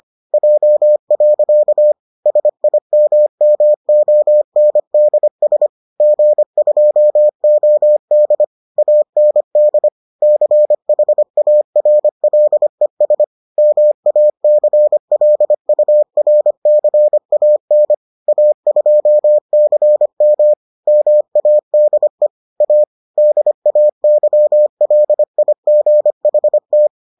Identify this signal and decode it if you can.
Morse code, heard as a keyed tone, about 25 wpm, 600 Hz. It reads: J. SIMMONDS G2OD AND CHARLES MACLURCAN A2CM MADE A DAYLIGHT